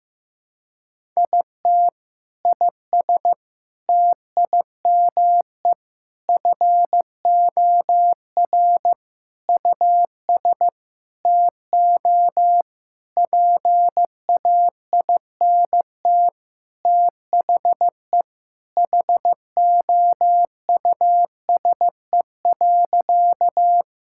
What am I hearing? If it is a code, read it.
Morse code, heard as a keyed tone, about 15 wpm, 700 Hz. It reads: IT IS TIME FOR US TO PAINT THE HOUSE.
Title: Sentences from Top 1000 Words - Review Part 6 - 15wpm